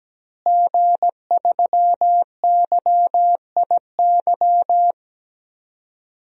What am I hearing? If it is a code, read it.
Morse code, heard as a keyed tone, about 17 wpm, 700 Hz. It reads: G3YIY
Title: DXCC-100 Call Signs - 5 Characters - Repeated 2x - 17wpm